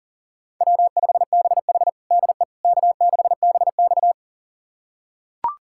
Morse code, heard as a keyed tone, about 40 wpm, 700 Hz. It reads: W5BH DE K6BX
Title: USA Callsign DE Callsign - 4 Characters - 40wpm